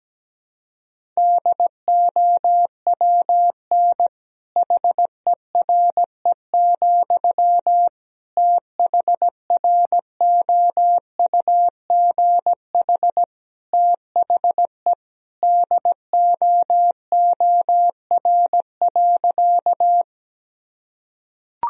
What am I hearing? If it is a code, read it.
Morse code, heard as a keyed tone, about 17 wpm, 700 Hz. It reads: DOWN HERE, THROUGH THE DOOR.
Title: Sentences from Top 1000 Words - Review Part 1 - 17wpm